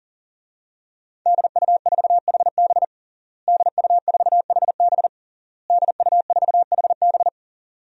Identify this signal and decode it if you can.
Morse code, heard as a keyed tone, about 37 wpm, 700 Hz. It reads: DU4HB DU4HB DU4HB
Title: DXCC-100 Call Signs - 5 Characters - Repeated 3x - 40wpm